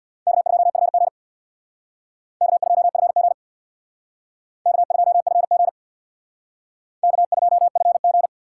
Morse code, heard as a keyed tone, about 63 wpm, 700 Hz. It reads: K2FG K2FG K2FG K2FG